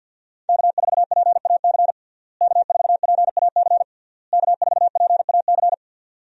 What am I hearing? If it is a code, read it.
Morse code, heard as a keyed tone, about 51 wpm, 700 Hz. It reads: K4PAC K4PAC K4PAC
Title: USA Call Signs - 5 Characters - Repeated 3x - 50wpm